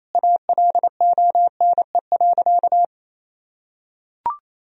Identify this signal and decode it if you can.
Morse code, heard as a keyed tone, about 27 wpm, 700 Hz. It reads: ALONE.
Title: Sentences from Top 1000 Words - Review Part 3 - 28wpm